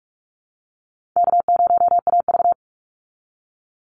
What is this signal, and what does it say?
Morse code, heard as a keyed tone, about 44 wpm, 700 Hz. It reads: K0AV
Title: Easy Call Signs Repeated 2x - 45wpm